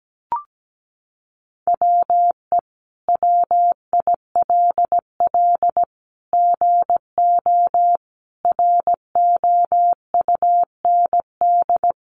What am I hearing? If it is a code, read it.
Morse code, heard as a keyed tone, about 17 wpm, 700 Hz. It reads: WE WILL GO ROUND
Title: Sentences from Top 1300 Words 17wpm